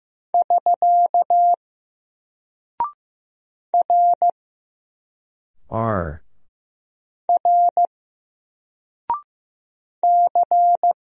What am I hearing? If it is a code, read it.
Morse code, heard as a keyed tone, about 15 wpm, 700 Hz. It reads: <SK> R R C
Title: Single Character (Letter:Number:Punctuation:Prosign) 15wpm